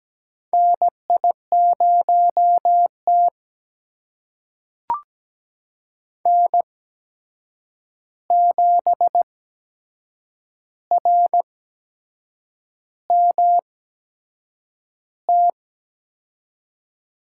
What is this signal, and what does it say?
Morse code, heard as a keyed tone, about 17 wpm, spaced as for 7 wpm, 700 Hz. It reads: NI0T N7RMT
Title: USA Call Signs - 8x Character Spacing - 17wpm